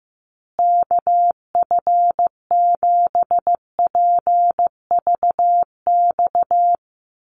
Morse code, heard as a keyed tone, about 15 wpm, 700 Hz. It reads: KF7PVX
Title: USA Call Signs Repeated 3x 15wpm